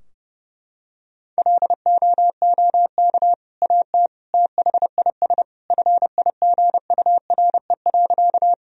Morse code, heard as a keyed tone, about 32 wpm, 700 Hz. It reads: LOOK AT THIS FIGURE.
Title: Sentences from Top 1000 Words - Review Part 2 - 30wpm